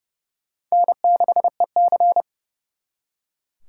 Morse code, heard as a keyed tone, about 38 wpm, 700 Hz. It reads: N6EC